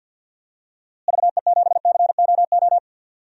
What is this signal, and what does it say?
Morse code, heard as a keyed tone, about 47 wpm, 700 Hz. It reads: VE7COO
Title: DXCC-100 Call Signs - 6 Characters - Repeated 3x - 50wpm